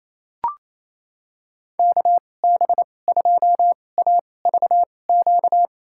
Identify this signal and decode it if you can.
Morse code, heard as a keyed tone, about 28 wpm, spaced as for 20 wpm, 700 Hz. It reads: KB2AVQ
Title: USA Call Signs - 6 Characters - 2x Character Spacing - 28wpm